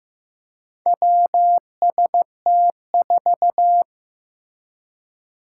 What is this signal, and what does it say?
Morse code, heard as a keyed tone, about 15 wpm, 700 Hz. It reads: WST4
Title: Four Challenging Letter-Number Combinations 15wpm